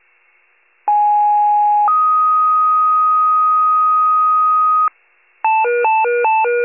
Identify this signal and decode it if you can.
Two-tone page, heard as a one-second tone, then a three-second tone, about 830 then 1250 Hz. It is heard from 0.9 to 4.9 s.